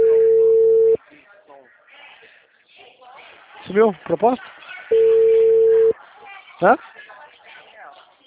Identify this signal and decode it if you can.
Ringback tone (China, 450 Hz), with speech.